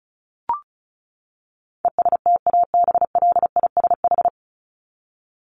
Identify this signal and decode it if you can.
Morse code, heard as a keyed tone, about 34 wpm, 700 Hz. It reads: ESTABLISH
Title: ED Suffix - Top 100 Words with Base Words 35wpm